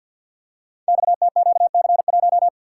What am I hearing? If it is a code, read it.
Morse code, heard as a keyed tone, about 52 wpm, 700 Hz. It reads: XTQC1
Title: Code Groups - 5 Letter:Numbers - 50wpm